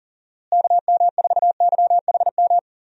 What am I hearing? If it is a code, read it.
Morse code, heard as a keyed tone, about 40 wpm, 700 Hz. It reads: KM4YHM
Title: USA Call Signs - 6 Characters - Repeated 3x - 40wpm